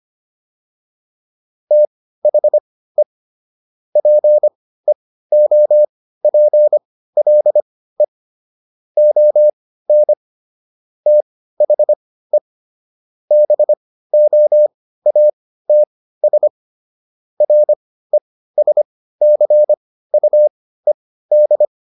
Morse code, heard as a keyed tone, about 25 wpm, spaced as for 15 wpm, 600 Hz. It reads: THE PEOPLE ON THE BOATS RESCUED